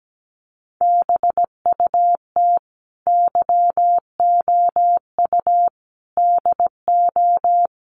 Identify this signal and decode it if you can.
Morse code, heard as a keyed tone, about 17 wpm, 700 Hz. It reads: BUT YOU DO
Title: Sets of 4 Words 17wpm